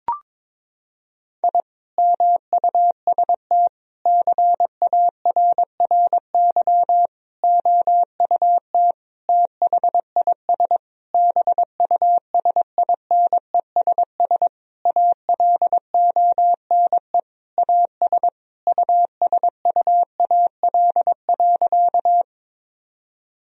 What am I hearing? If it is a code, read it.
Morse code, heard as a keyed tone, about 22 wpm, 700 Hz. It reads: I MUST CARRY OUT THIS BUSINESS ALONE AS USUAL.
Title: Sentences from Top 1000 Words - Review Part 2 - 22wpm